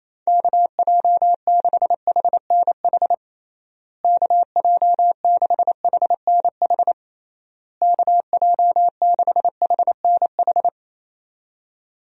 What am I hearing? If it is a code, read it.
Morse code, heard as a keyed tone, about 28 wpm, 700 Hz. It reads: KJ6HNH KJ6HNH KJ6HNH
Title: USA Call Signs - 6 Characters - Repeated 3x - 28wpm